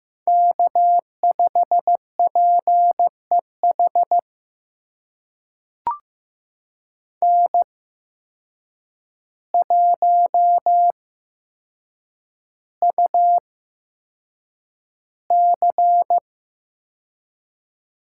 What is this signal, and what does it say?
Morse code, heard as a keyed tone, about 15 wpm, spaced as for 7 wpm, 700 Hz. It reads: K5PEH N1UC
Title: USA Call Signs - 5 Characters - 8x Character Spacing - 15wpm